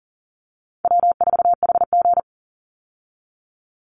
Morse code, heard as a keyed tone, about 40 wpm, 700 Hz. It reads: W4HG
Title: Easy Call Signs Repeated 3x - 40wpm